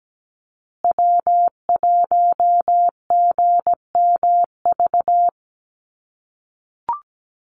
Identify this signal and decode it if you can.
Morse code, heard as a keyed tone, about 17 wpm, 700 Hz. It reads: W1GMV
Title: USA Call Signs Repeated 3x 17wpm